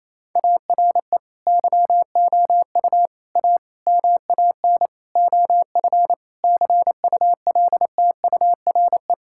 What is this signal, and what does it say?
Morse code, heard as a keyed tone, about 28 wpm, 700 Hz. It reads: ARE YOU A MAN OF CULTURE
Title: Sentences from Top 1000 Words - Review Part 6 - 28wpm